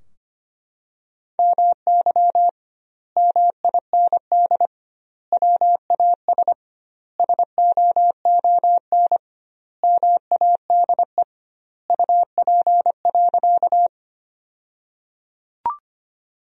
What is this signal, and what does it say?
Morse code, heard as a keyed tone, about 25 wpm, 700 Hz. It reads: MY MIND WAS SOON MADE UP.
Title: 2x Word-Spacing - Sentences from Top 500 Words - 25wpm